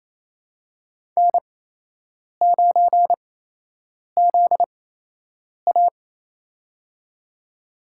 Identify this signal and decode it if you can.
Morse code, heard as a keyed tone, about 28 wpm, spaced as for 8 wpm, 700 Hz. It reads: N9ZA
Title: USA Call Signs - 4 Characters - 8x Character Spacing - 28wpm